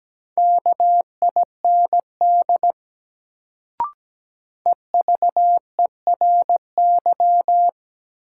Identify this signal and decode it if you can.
Morse code, heard as a keyed tone, about 17 wpm, 700 Hz. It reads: KIND EVERY
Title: Sentences from Top 200 Words - New Word Drill - 17wpm